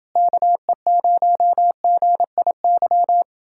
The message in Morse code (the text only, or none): KE0GIY